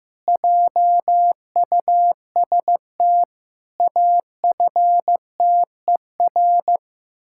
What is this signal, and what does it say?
Morse code, heard as a keyed tone, about 15 wpm, 700 Hz. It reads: JUST AFTER
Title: Pangram Challenge v2 - 15wpm